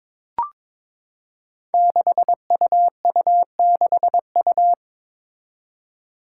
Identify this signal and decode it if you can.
Morse code, heard as a keyed tone, about 22 wpm, 700 Hz. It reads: 6UU6U
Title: Five Challenging Letter-Number Combinations 22wpm